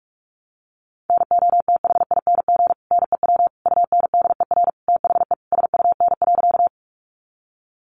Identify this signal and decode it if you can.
Morse code, heard as a keyed tone, about 44 wpm, 700 Hz. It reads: NOTHING NEW UNDER THE SUN.
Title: Sentences from Top 500 Words 45wpm